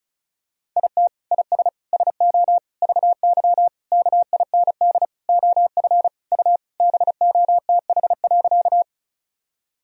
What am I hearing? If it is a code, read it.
Morse code, heard as a keyed tone, about 36 wpm, 700 Hz. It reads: IT IS SO VY KIND OF U BOTH.
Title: Sentences from Top 300 Words and Abbreviations 35wpm